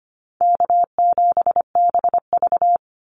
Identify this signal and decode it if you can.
Morse code, heard as a keyed tone, about 25 wpm, 700 Hz. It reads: K7BV
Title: Call Signs - 25wpm